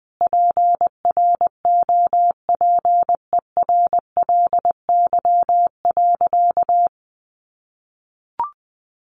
Morse code, heard as a keyed tone, about 20 wpm, 700 Hz. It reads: PROPERLY.
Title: Sentences from Top 1100 Words 20wpm